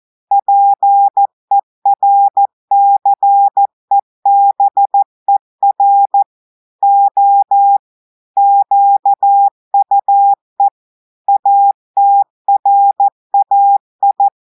Morse code, heard as a keyed tone, about 14 wpm, 800 Hz. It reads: PERCEBER O QUE ATRAI
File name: audios / portugais / portugais texte en clair 2 vitesse 14 mots minutes